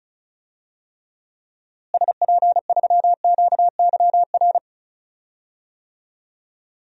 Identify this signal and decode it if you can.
Morse code, heard as a keyed tone, about 36 wpm, 700 Hz. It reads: SP3QYR
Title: DXCC-100 Call Signs - 6 Characters - Repeated 3x - 35wpm